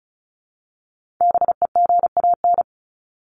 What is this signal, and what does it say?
Morse code, heard as a keyed tone, about 34 wpm, 700 Hz. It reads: BEGAN
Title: Top 300 Words 35wpm